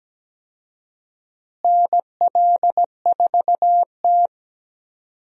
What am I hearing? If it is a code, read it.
Morse code, heard as a keyed tone, about 17 wpm, 700 Hz. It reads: NL4T
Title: USA Call Signs - 4 Characters - Repeated 3x - 17wpm